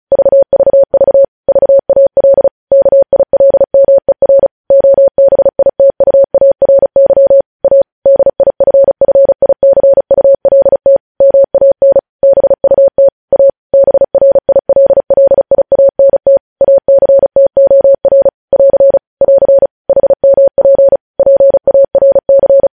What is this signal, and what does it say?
Morse code, heard as a keyed tone, about 36 wpm, 550 Hz. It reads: VVV VAL KILMER OBITUARY A DIFFICULT MAN BUT A BRILLIANT ACTOR <AR> <AR> HMP PARC
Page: News Headlines in Morse Code at 35 WPM